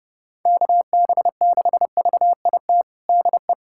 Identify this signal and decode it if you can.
Morse code, heard as a keyed tone, about 31 wpm, 700 Hz. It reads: KB6VIT DE
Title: USA Callsign DE Callsign - 6 Characters - 30wpm